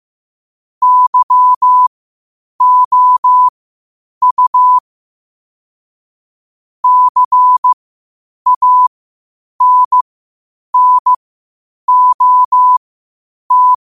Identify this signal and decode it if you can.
Morse code, heard as a keyed tone, about 15 wpm, spaced as for 8 wpm, 1000 Hz. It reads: YOU CANNOT
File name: qotdfiles / 8wpm